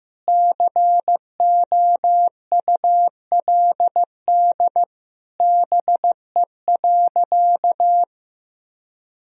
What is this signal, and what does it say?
Morse code, heard as a keyed tone, about 15 wpm, 700 Hz. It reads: COULD BE.
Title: Sentences from Top 1000 Words - Review Part 3 - 15wpm